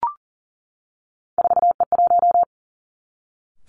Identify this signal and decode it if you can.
Morse code, heard as a keyed tone, about 41 wpm, 700 Hz. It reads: 4E1
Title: Three Challenging Letter-Number Combinations 40wpm